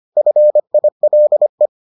Morse code, heard as a keyed tone, about 25 wpm, 600 Hz. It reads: FILE